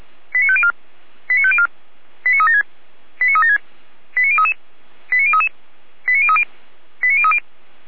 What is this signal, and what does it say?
Five-tone selective call (ZVEI), heard as quick runs of five tones, tones of about 70 ms. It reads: 89504 89504 89376 89376 89033 89033 89030 89030
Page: /455/SDR/ZVEI/89070/